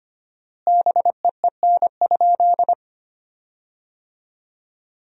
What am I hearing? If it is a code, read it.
Morse code, heard as a keyed tone, about 25 wpm, 700 Hz. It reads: BEEN?
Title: 4x Word-Spacing - Sentences from Top 400 Words - 25wpm